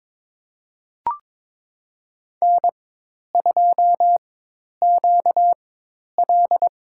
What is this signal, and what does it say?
Morse code, heard as a keyed tone, about 22 wpm, spaced as for 10 wpm, 700 Hz. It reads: N2QL
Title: USA Call Signs - 4 Characters - 4x Character Spacing - 22wpm